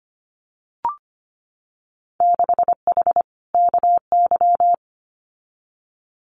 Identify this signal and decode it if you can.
Morse code, heard as a keyed tone, about 25 wpm, 700 Hz. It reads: 6H KY
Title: Winter Field Day Exchanges Encore 25wpm